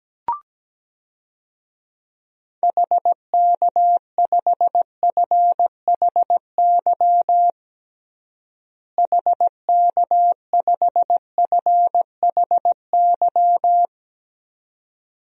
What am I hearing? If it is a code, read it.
Morse code, heard as a keyed tone, about 17 wpm, 700 Hz. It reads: HK5FHY HK5FHY
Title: DXCC-100 Call Signs - 6 Characters - Repeated 2x - 17wpm